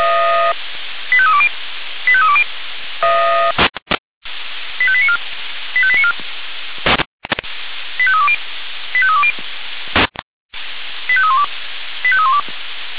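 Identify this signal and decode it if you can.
Five-tone selective call (ZVEI), heard as quick runs of five tones, tones of about 70 ms. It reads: 85320 85320 85894 85894 85320 85320 85312 85312
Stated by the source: ZVEI five-tone selective call received with an SDR.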